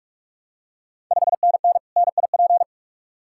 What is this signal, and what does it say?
Morse code, heard as a keyed tone, about 44 wpm, 700 Hz. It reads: 5NN NIP
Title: Ontario QSO Party Exchange 45wpm